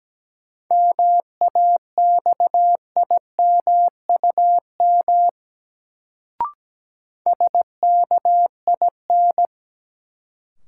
Morse code, heard as a keyed tone, about 17 wpm, 700 Hz. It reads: MAXIMUM SKIN
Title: Sentences from Top 1300 Words - New Word Drill Alt - 17wpm